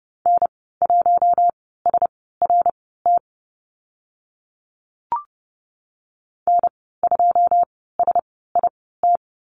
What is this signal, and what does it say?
Morse code, heard as a keyed tone, about 30 wpm, spaced as for 17 wpm, 700 Hz. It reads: N1SRT N2SIT
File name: Lesson 14g - Number 2 - Call Signs 30wpm